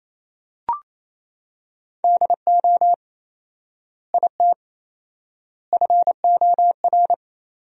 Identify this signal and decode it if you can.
Morse code, heard as a keyed tone, about 28 wpm, 700 Hz. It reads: DO IT FOR
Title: 4x Word-Spacing - Sentences from Top 200 Words - 28wpm